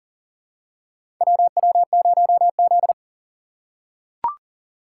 Morse code, heard as a keyed tone, about 40 wpm, 700 Hz. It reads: WW0Z
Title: USA Call Signs - 4 Characters - 6x Character Spacing - 40wpm